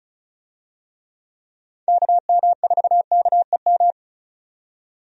Morse code, heard as a keyed tone, about 34 wpm, 700 Hz. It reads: KM4KEM
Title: USA Call Signs - 6 Characters - 35wpm